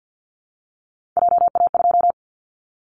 Morse code, heard as a keyed tone, about 50 wpm, 700 Hz. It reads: JA2
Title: Callsign Prefixes for Top 50 DXCC - 50wpm